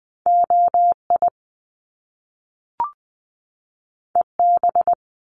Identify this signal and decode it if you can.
Morse code, heard as a keyed tone, about 20 wpm, 700 Hz. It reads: OI EB
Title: Top Two Letter Combinations in Words 20wpm